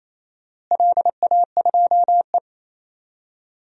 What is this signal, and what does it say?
Morse code, heard as a keyed tone, about 28 wpm, 700 Hz. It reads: LA2E
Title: DXCC-100 Call Signs - 4 Characters - Repeated 3x - 28wpm